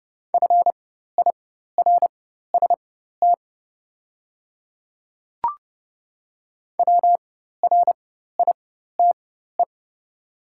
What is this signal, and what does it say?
Morse code, heard as a keyed tone, about 30 wpm, spaced as for 14 wpm, 690 Hz. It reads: FIRST WRITE
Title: 4x Character Spacing - Top 100 Words - 30wpm